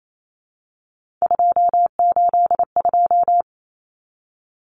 Morse code, heard as a keyed tone, about 28 wpm, 700 Hz. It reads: 282
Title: Numbers - 3-Digit 28wpm